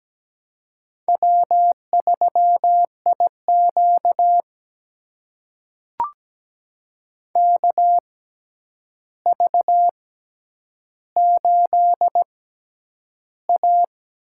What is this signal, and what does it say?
Morse code, heard as a keyed tone, about 17 wpm, spaced as for 9 wpm, 700 Hz. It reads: W3IQ KV8A
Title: USA Call Signs - 4 Characters - 6x Character Spacing - 17wpm